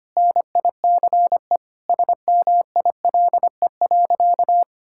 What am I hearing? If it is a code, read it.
Morse code, heard as a keyed tone, about 25 wpm, 700 Hz. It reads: NICE SMILE.
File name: practice / 25wpm / Sentences with 2-Word Phrases - 25wpm